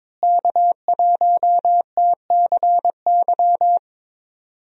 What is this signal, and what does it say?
Morse code, heard as a keyed tone, about 22 wpm, 700 Hz. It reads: K1TCY
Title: USA Call Signs - 5 Characters - Repeated 3x - 22wpm